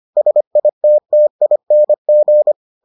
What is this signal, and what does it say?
Morse code, heard as a keyed tone, about 25 wpm, 600 Hz. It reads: SITTING